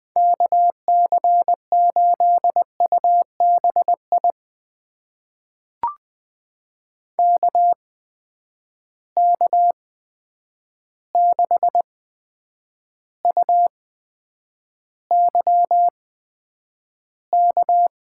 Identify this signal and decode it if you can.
Morse code, heard as a keyed tone, about 20 wpm, spaced as for 9 wpm, 700 Hz. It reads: KC8UBI KK6UYK
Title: USA Call Signs - 6 Characters - 8x Character Spacing - 20wpm